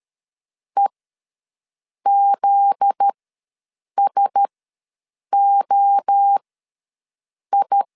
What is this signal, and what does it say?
Morse code, heard as a keyed tone, about 13 wpm, spaced as for 6 wpm, 800 Hz. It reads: EZSOI
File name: LEZIONI / 8 YJ45 / 8 Y ws 12wpm 3409